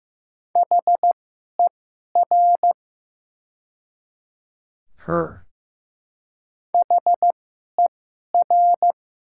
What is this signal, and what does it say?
Morse code, heard as a keyed tone, about 15 wpm, spaced as for 11 wpm, 700 Hz. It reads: HER HER
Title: 2x Character Spacing - Top 500 Words - 15wpm